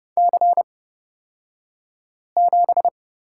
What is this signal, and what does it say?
Morse code, heard as a keyed tone, about 30 wpm, spaced as for 5 wpm, 700 Hz. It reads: C7
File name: Lesson 36d - BT - Two Characters 30wpm